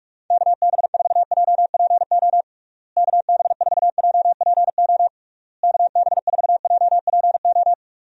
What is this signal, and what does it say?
Morse code, heard as a keyed tone, about 42 wpm, 700 Hz. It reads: KB4JPO KB4JPO KB4JPO
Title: USA Call Signs - 6 Characters - Repeated 3x - 45wpm